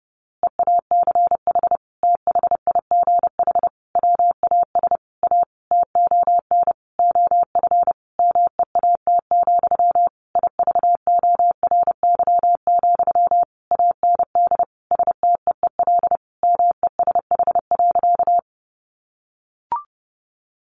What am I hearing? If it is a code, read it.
Morse code, heard as a keyed tone, about 28 wpm, 700 Hz. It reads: EACH THIGH WAS A TON OF MEAT, IVORY, AND STEEL MESH.
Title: A Sound Of Thunder 30wpm